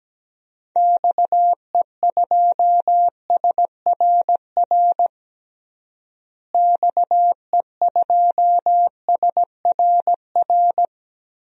Morse code, heard as a keyed tone, about 17 wpm, 700 Hz. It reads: XE2SRR XE2SRR
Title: DXCC-100 Call Signs - 6 Characters - Repeated 3x - 17wpm